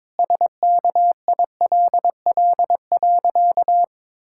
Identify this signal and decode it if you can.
Morse code, heard as a keyed tone, about 22 wpm, 700 Hz. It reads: SKILL.
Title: Pangram Challenge v2 - 22wpm